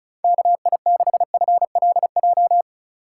Morse code, heard as a keyed tone, about 36 wpm, 700 Hz. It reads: KI6FLJ
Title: USA Call Signs - 6 Characters - 2x Character Spacing - 35wpm